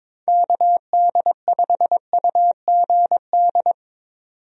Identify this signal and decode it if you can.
Morse code, heard as a keyed tone, about 22 wpm, 700 Hz. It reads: KD5UGD
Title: USA Call Signs - 6 Characters - 22wpm